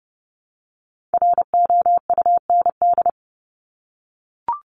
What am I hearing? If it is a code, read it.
Morse code, heard as a keyed tone, about 30 wpm, 700 Hz. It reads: ROUND
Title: Top 200 Words Repeated 30wpm